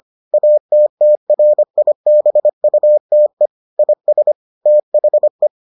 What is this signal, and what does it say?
Morse code, heard as a keyed tone, about 25 wpm, 600 Hz. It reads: ATTRIBUTE IS THE